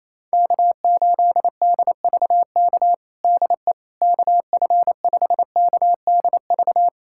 Morse code, heard as a keyed tone, about 27 wpm, 700 Hz. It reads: K8DVK DE KF5KDV
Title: USA Callsign DE Callsign 28wpm